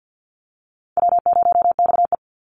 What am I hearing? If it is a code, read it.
Morse code, heard as a keyed tone, about 50 wpm, 700 Hz. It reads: W0XE
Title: Easy Call Signs Repeated 3x - 50wpm